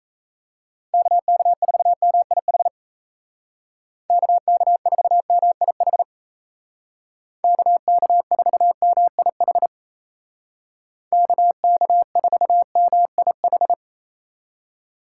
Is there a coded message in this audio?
Yes: Morse code from 0.9 to 13.7 s, heard as a keyed tone, about 36 wpm, 700 Hz. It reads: KK4MIH KK4MIH KK4MIH KK4MIH